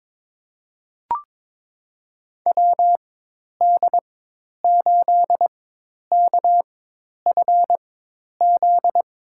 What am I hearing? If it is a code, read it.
Morse code, heard as a keyed tone, about 21 wpm, spaced as for 10 wpm, 700 Hz. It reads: WD8KFZ
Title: USA Call Signs - 6 Characters - 4x Character Spacing - 22wpm